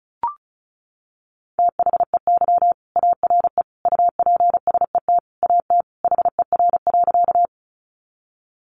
Morse code, heard as a keyed tone, about 34 wpm, 700 Hz. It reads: THEY ARE UPSET AT HER.
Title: Sentences from Top 800 Words 35wpm